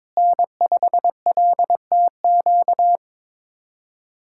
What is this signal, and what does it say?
Morse code, heard as a keyed tone, about 22 wpm, 700 Hz. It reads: N5LTQ
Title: USA Callsign DE Callsign - 5 Characters - 22wpm